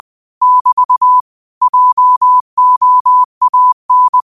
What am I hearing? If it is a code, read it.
Morse code, heard as a keyed tone, about 20 wpm, 1000 Hz. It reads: <BT> JOAN